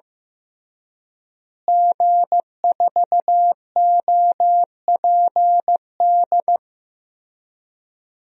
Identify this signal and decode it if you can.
Morse code, heard as a keyed tone, about 16 wpm, 700 Hz. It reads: G4OPD
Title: DXCC-100 Call Signs - 5 Characters - Repeated 3x - 15wpm